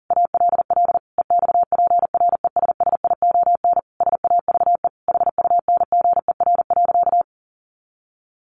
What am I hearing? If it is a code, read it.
Morse code, heard as a keyed tone, about 40 wpm, 700 Hz. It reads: ALL EXPRESSION SAVE HUNGER.
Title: A Sound Of Thunder 40wpm